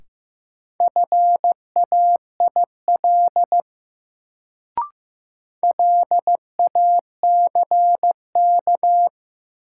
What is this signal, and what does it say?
Morse code, heard as a keyed tone, about 15 wpm, 700 Hz. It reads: FAIL LACK
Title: Sentences from Top 1100 Words - New Word Drill - 15wpm